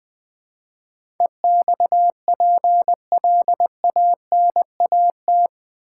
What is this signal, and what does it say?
Morse code, heard as a keyed tone, about 20 wpm, 700 Hz. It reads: EXPLANAT
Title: Build A Word - 4 Letters At A Time - 11-Letter Words - 20wpm